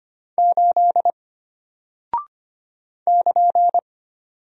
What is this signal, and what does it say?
Morse code, heard as a keyed tone, about 25 wpm, 700 Hz. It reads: 8 <KN>
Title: Single Character (Letter:Number:Punctuation:Prosign) Alt 25wpm